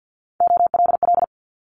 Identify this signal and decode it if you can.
Morse code, heard as a keyed tone, about 50 wpm, 700 Hz. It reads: OLL
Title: Multi-letter Phonemes Quick Fire 50wpm